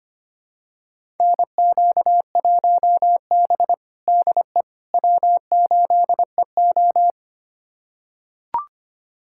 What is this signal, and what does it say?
Morse code, heard as a keyed tone, about 25 wpm, 700 Hz. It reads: NQ1B DE W8EO
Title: USA Callsign DE Callsign - 4 Characters - 25wpm